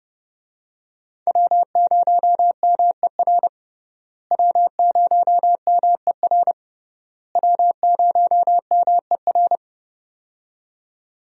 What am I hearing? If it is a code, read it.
Morse code, heard as a keyed tone, about 30 wpm, 700 Hz. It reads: W0MER W0MER W0MER
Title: USA Call Signs - 5 Characters - Repeated 3x - 30wpm